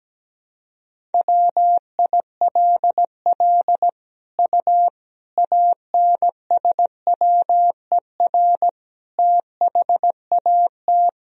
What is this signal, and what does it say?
Morse code, heard as a keyed tone, about 17 wpm, 700 Hz. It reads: WILL U ANSWER THAT
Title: Sentences from Top 400 Words and Abbreviations 17wpm